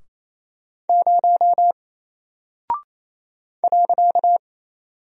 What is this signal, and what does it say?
Morse code, heard as a keyed tone, about 27 wpm, 700 Hz. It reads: 0 .
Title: Single Character (Letter:Number:Punctuation:Prosign) Alt 28wpm